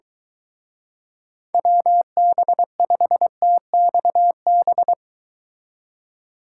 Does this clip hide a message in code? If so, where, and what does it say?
Yes: Morse code from 1.5 to 4.9 s, heard as a keyed tone, about 22 wpm, 700 Hz. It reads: WB5TXB